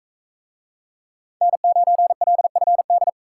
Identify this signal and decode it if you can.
Morse code, heard as a keyed tone, about 39 wpm, 700 Hz. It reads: N9LFD